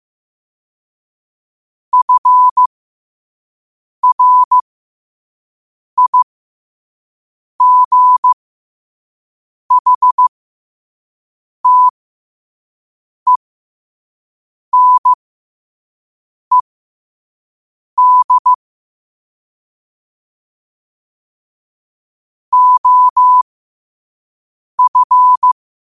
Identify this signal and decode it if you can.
Morse code, heard as a keyed tone, about 15 wpm, spaced as for 5 wpm, 1000 Hz. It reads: FRIGHTENED OF
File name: qotdfiles / 5wpm